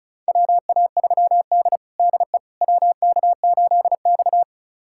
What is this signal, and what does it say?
Morse code, heard as a keyed tone, about 34 wpm, 700 Hz. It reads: WA3D DE WK8X
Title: USA Callsign DE Callsign - 4 Characters - 35wpm